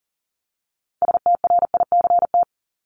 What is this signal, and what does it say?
Morse code, heard as a keyed tone, about 40 wpm, 700 Hz. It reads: STRICT
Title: Top 1501-1750 Words 40wpm